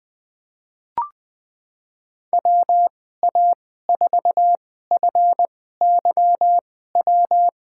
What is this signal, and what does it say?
Morse code, heard as a keyed tone, about 20 wpm, spaced as for 15 wpm, 700 Hz. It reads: WA4FYW
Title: USA Call Signs - 6 Characters - 2x Character Spacing - 20wpm